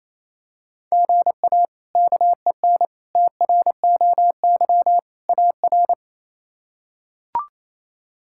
Morse code, heard as a keyed tone, about 28 wpm, 700 Hz. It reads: GA KEN TROY AR
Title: K1USN SST QSO Practice - 28wpm